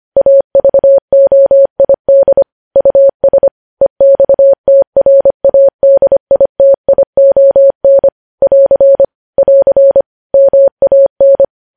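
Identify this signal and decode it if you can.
Morse code, heard as a keyed tone, about 25 wpm, 550 Hz. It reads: AVOID US EXTRADITION <AR> <AR> MAN
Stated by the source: News headlines sent in Morse code at 25.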